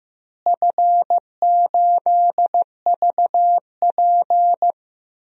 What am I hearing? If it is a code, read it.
Morse code, heard as a keyed tone, about 16 wpm, 700 Hz. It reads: F8VP
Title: DXCC-100 Call Signs - 4 Characters - Repeated 3x - 15wpm